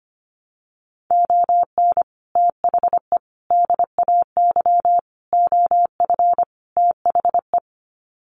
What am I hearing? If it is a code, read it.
Morse code, heard as a keyed tone, about 25 wpm, 700 Hz. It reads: ON THE DAY OF THE
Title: Sets of 5 Words Encore 25wpm